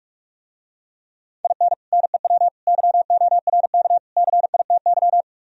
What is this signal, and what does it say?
Morse code, heard as a keyed tone, about 48 wpm, 700 Hz. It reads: IN NEW YORK CITY